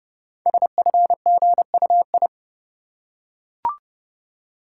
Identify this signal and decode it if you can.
Morse code, heard as a keyed tone, about 30 wpm, 700 Hz. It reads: SFGUI
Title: Code Groups - 5 Letter:Numbers - 30wpm